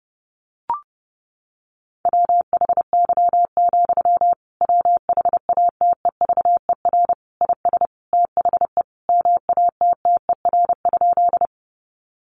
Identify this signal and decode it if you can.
Morse code, heard as a keyed tone, about 28 wpm, 700 Hz. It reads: WHY, WHATEVER IS THE MATTER?
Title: Sentences from Top 1100 Words 30wpm